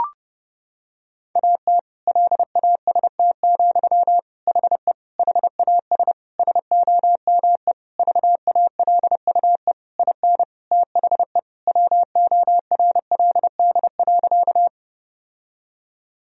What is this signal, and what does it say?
Morse code, heard as a keyed tone, about 30 wpm, 700 Hz. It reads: AT LAST, HE HAS SOME VALUE IN THE WORLD.
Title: Sentences from Top 1000 Words - Review Part 3 - 30wpm